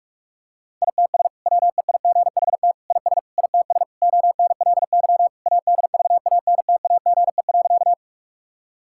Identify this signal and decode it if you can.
Morse code, heard as a keyed tone, about 46 wpm, 700 Hz. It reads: ITS WEIGHT IS ITS ONLY ADVANTAGE.
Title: Sentences from Top 1000 Words - Review Part 6 - 45wpm